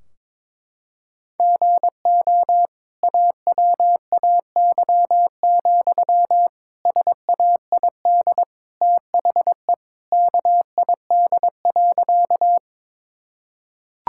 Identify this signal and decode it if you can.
Morse code, heard as a keyed tone, about 22 wpm, 700 Hz. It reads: GO AWAY, SAID THE KID.
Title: Sentences from Top 1600 Words 22wpm